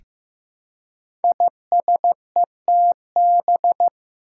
Morse code, heard as a keyed tone, about 15 wpm, 700 Hz. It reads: ISETB
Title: Five Challenging Letter-Number Combinations 15wpm